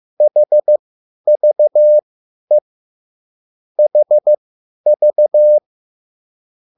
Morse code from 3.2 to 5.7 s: HV